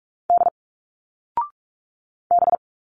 Morse code, heard as a keyed tone, about 43 wpm, 700 Hz. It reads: D B
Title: Single Letters 45wpm